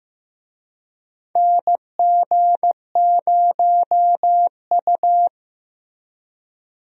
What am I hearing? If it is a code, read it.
Morse code, heard as a keyed tone, about 15 wpm, 700 Hz. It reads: NG0U
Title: USA Call Signs - 4 Characters - Repeated 3x - 15wpm